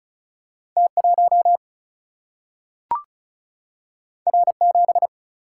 Morse code, heard as a keyed tone, about 36 wpm, 700 Hz. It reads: T1 R7
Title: Single-Letter Single-Number 35wpm